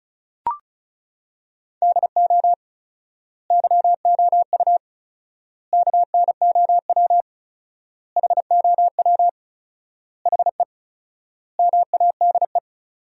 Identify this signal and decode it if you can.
Morse code, heard as a keyed tone, about 36 wpm, 700 Hz. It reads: DO YOU KNOW HOW HE MADE
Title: 4x Word-Spacing - Sentences from Top 300 Words - 35wpm